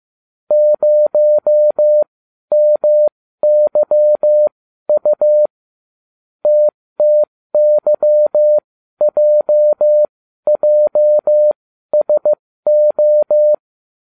Morse code, heard as a keyed tone, about 15 wpm, spaced as for 12 wpm, 600 Hz. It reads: 0MYU TTYJJSO